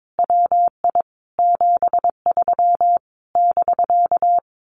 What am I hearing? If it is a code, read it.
Morse code, heard as a keyed tone, about 22 wpm, 700 Hz. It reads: WI 73 <BK>
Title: Common QSO Phrases Encore 22wpm